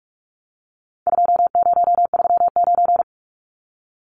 Morse code, heard as a keyed tone, about 44 wpm, 700 Hz. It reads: 2039
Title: Numbers - 4-Digit 45wpm